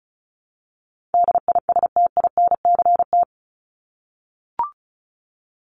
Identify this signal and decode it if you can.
Morse code, heard as a keyed tone, about 34 wpm, 700 Hz. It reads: DISTINCT
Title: Top 1501-1750 Words 35wpm